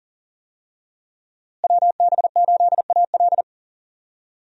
Morse code, heard as a keyed tone, about 40 wpm, 700 Hz. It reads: WB8AL
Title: USA Call Signs - 5 Characters - Repeated 2x - 40wpm